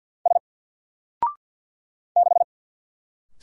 Morse code, heard as a keyed tone, about 50 wpm, 700 Hz. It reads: S 6
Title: Single Letter-Number 50wpm